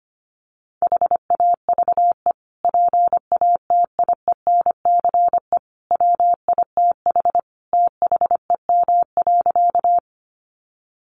Morse code, heard as a keyed tone, about 25 wpm, 700 Hz. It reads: HAVE PATIENCE WITH THEM.
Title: Sentences from Top 1300 Words 25wpm